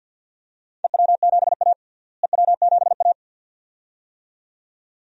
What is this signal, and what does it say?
Morse code, heard as a keyed tone, about 50 wpm, 700 Hz. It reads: EW7A EW7A
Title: DXCC-100 Call Signs - 4 Characters - Repeated 2x - 50wpm